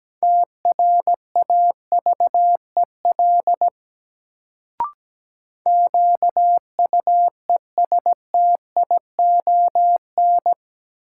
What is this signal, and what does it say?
Morse code, heard as a keyed tone, about 17 wpm, 700 Hz. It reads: TRAVEL QUESTION
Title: Sentences from Top 400 Words - New Word Drill - 17wpm